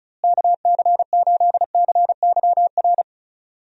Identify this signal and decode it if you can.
Morse code, heard as a keyed tone, about 34 wpm, 700 Hz. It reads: KC8CYR
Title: USA Call Signs - 6 Characters - Repeated 3x - 35wpm